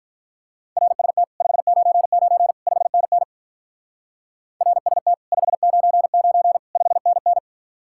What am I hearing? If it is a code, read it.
Morse code, heard as a keyed tone, about 44 wpm, 700 Hz. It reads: RST 599 5NN RST 599 5NN